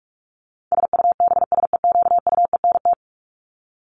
Morse code, heard as a keyed tone, about 44 wpm, 700 Hz. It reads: SUBSEQUENT
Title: LY Suffix - Top 100 Words with Base Words 45wpm